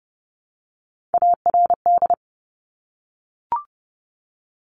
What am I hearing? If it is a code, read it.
Morse code, heard as a keyed tone, about 30 wpm, 690 Hz. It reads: ARD
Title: Top Three Letter Combinations in Words 30wpm